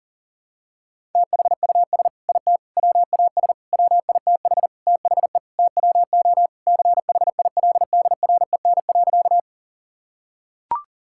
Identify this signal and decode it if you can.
Morse code, heard as a keyed tone, about 37 wpm, 700 Hz. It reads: THUS IT WAS WITH THE TWO CHILDREN.
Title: Sentences from Top 1000 Words - Review Part 5 - 40wpm